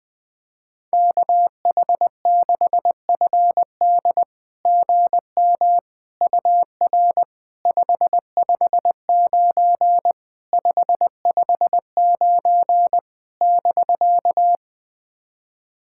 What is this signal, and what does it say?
Morse code, heard as a keyed tone, about 20 wpm, 690 Hz. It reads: KH6FD GM UR 559 559 <BK>
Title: SOTA QSO Practice 20wpm